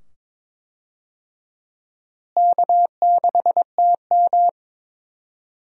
Morse code, heard as a keyed tone, about 22 wpm, 700 Hz. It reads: K6TM
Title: USA Call Signs - 4 Characters - Repeated 2x - 22wpm